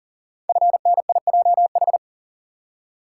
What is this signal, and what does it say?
Morse code, heard as a keyed tone, about 40 wpm, 700 Hz. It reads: FNIJH
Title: Code Groups - 5 Letters - 40wpm